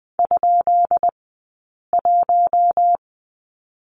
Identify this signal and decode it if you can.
Morse code, heard as a keyed tone, about 20 wpm, spaced as for 8 wpm, 700 Hz. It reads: ?1
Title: 217 Lesson 35d - Number 0 - Two Characters 20wpm